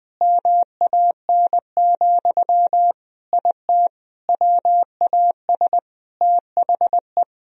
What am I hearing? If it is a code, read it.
Morse code, heard as a keyed tone, about 20 wpm, 700 Hz. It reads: MAN, IT WAS THE
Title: Sentences from Top 1000 Words - Review Part 4 - 20wpm